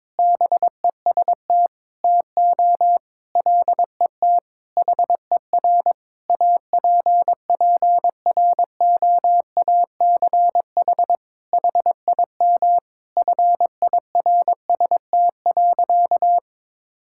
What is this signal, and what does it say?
Morse code, heard as a keyed tone, about 22 wpm, 700 Hz. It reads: BEST TO LET HER APPROACH HIM FIRST.
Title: Sentences from Top 1000 Words - Review Part 6 - 22wpm